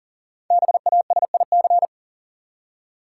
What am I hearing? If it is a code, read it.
Morse code, heard as a keyed tone, about 37 wpm, 700 Hz. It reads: BASIC
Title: Five Letter Words - 40wpm